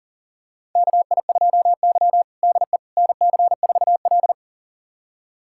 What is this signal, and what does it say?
Morse code, heard as a keyed tone, about 37 wpm, 700 Hz. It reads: KI2Y DE NC4L
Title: USA Callsign DE Callsign - 4 Characters - 40wpm